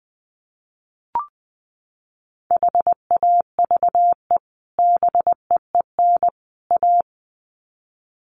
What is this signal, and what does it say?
Morse code, heard as a keyed tone, about 20 wpm, 700 Hz. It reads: HAVE BEEN A
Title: Sets of 3 Words 20wpm